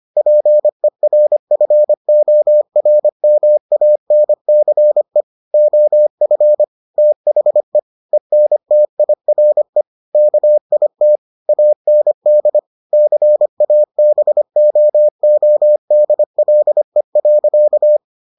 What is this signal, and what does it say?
Morse code, heard as a keyed tone, about 25 wpm, 600 Hz. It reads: PERFORMANCE OF THE ENTIRE KIT AND CABOODLE.